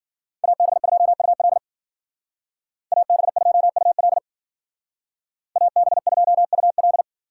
Morse code, heard as a keyed tone, about 52 wpm, 700 Hz. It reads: AB2UL AB2UL AB2UL